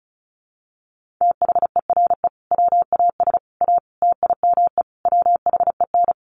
Morse code, heard as a keyed tone, about 34 wpm, 700 Hz. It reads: THERE WAS A TIME WHEN
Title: Sets of 5 Words 35wpm